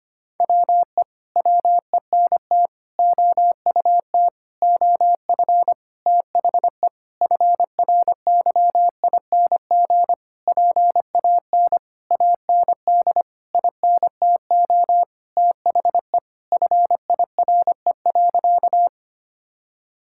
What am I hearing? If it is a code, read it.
Morse code, heard as a keyed tone, about 25 wpm, 700 Hz. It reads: WE WENT OUT OF THE FRYING PAN AND INTO THE FIRE.
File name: practice / 25wpm / Sentences with 9-Word Phrases - 25wpm